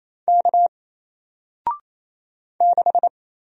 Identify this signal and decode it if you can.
Morse code, heard as a keyed tone, about 28 wpm, 700 Hz. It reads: K 6
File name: Single-Letter-Number-28wpm